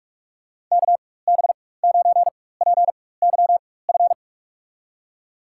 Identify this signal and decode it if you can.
Morse code, heard as a keyed tone, about 42 wpm, spaced as for 20 wpm, 700 Hz. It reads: KB9PYF
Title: USA Call Signs - 6 Characters - 4x Character Spacing - 45wpm